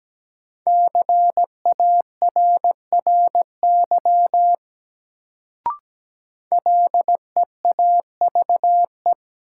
Morse code, heard as a keyed tone, about 17 wpm, 700 Hz. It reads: CARRY LEAVE
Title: Sentences from Top 400 Words - New Word Drill - 17wpm